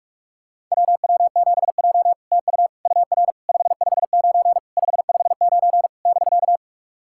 Morse code, heard as a keyed tone, about 44 wpm, 690 Hz. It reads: WW7J TU UR 559 559 <BK>
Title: SOTA QSO Practice 45wpm